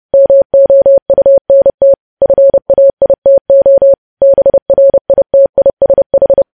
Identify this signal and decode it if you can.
Morse code, heard as a keyed tone, about 30 wpm, 550 Hz. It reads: MOUNT FAITO BRITISH
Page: News Headlines in Morse Code at 30 WPM